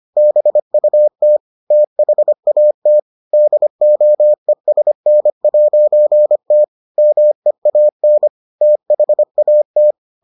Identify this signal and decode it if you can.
Morse code, heard as a keyed tone, about 25 wpm, 600 Hz. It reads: BUT THAT DOESN'T MEAN THAT